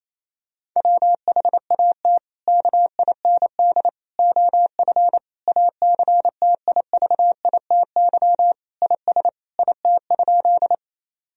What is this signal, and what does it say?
Morse code, heard as a keyed tone, about 28 wpm, 700 Hz. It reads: WHAT KIND OF ACTIVITY IS IT?
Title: Sentences from Top 1000 Words - Review Part 4 - 28wpm